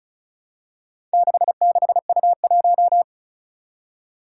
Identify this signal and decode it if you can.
Morse code, heard as a keyed tone, about 36 wpm, 700 Hz. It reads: 66U1
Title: Four Challenging Letter-Number Combinations 35wpm